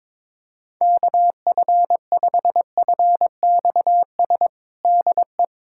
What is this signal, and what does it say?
Morse code, heard as a keyed tone, about 22 wpm, 700 Hz. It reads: KF5FXS DE
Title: USA Callsign DE Callsign - 6 Characters - 22wpm